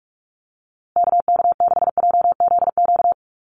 Morse code, heard as a keyed tone, about 44 wpm, 700 Hz. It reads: KK6JZQ
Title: USA Call Signs Repeated 2x 45wpm